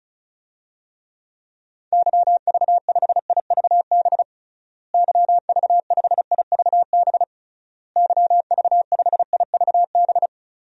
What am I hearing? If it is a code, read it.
Morse code, heard as a keyed tone, about 36 wpm, 700 Hz. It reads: YV5IVB YV5IVB YV5IVB
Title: DXCC-100 Call Signs - 6 Characters - Repeated 3x - 35wpm